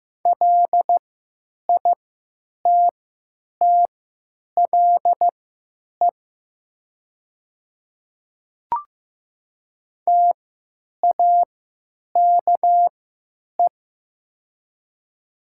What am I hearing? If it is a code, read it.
Morse code, heard as a keyed tone, about 15 wpm, spaced as for 9 wpm, 700 Hz. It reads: LITTLE TAKE
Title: 3x Character Spacing - Top 300 Words - 15wpm